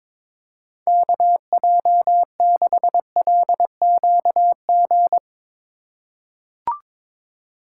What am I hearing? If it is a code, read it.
Morse code, heard as a keyed tone, about 22 wpm, 700 Hz. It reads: KJ6LQG
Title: USA Call Signs - 6 Characters - 2x Character Spacing - 22wpm